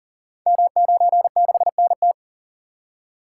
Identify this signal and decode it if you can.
Morse code, heard as a keyed tone, about 42 wpm, 700 Hz. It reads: M96NT
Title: Code Groups - 5 Letter:Numbers - 40wpm